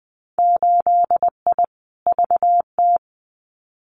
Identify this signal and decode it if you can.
Morse code, heard as a keyed tone, about 20 wpm, 700 Hz. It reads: 8I VT
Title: Winter Field Day Exchanges Encore 20wpm